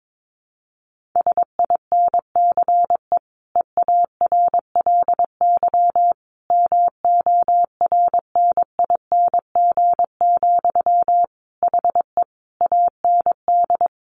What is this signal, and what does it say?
Morse code, heard as a keyed tone, about 22 wpm, 700 Hz. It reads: SINCE EARLY MORNING, HE AND
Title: Sentences from Top 1400 Words 22wpm